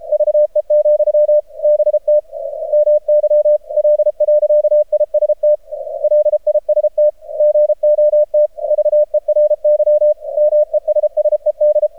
Morse code, heard as a keyed tone, about 30 wpm, 600 Hz. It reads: VE, BT MY L.IST LIST GOT VERY MESSED
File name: archived / CRSnet